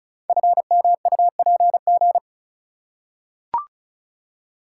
Morse code, heard as a keyed tone, about 36 wpm, 690 Hz. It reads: FMUPG
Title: Code Groups - 5 Letters - 35wpm